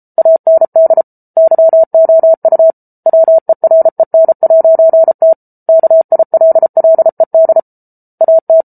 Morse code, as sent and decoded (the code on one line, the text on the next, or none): .- -. -.. / -.-- --- ..- / .-- . .-. -. .----. - / -.- .. .-.. .-.. -.. / .- -
AND YOU WEREN'T KILLED AT